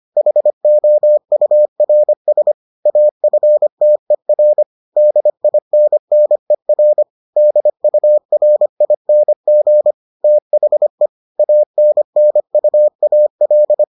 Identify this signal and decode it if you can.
Morse code, heard as a keyed tone, about 25 wpm, 600 Hz. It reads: HOURS AFTER DINNER DURING THE ANNUAL